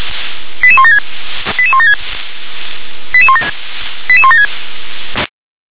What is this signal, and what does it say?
Five-tone selective call (ZVEI), heard as quick runs of five tones, tones of about 70 ms. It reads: 88176 88176 88176